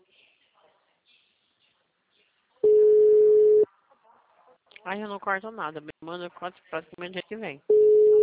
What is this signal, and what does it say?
Ringback tone (Europe, 425 Hz), with speech.